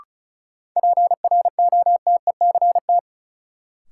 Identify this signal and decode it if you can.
Morse code, heard as a keyed tone, about 34 wpm, 700 Hz. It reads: PROTECT
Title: Sentences from Top 900 Words - New Word Drill - 35wpm